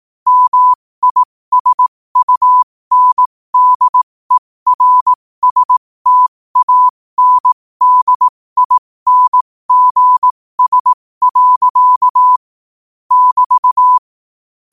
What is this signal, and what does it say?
Morse code, heard as a keyed tone, about 18 wpm, spaced as for 15 wpm, 1000 Hz. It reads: MISUNDERSTANDINGS. <BT>